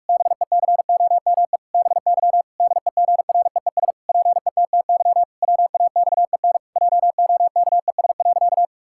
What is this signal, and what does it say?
Morse code, heard as a keyed tone, about 46 wpm, 700 Hz. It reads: BECOME BY DEGREES PETTY WAXEN JOKES.
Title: Pangram Challenge v2 - 45wpm